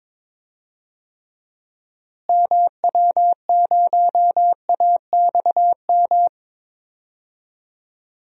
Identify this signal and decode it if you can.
Morse code, heard as a keyed tone, about 21 wpm, 700 Hz. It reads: MW0AXM
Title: DXCC-100 Call Signs - 6 Characters - Repeated 2x - 22wpm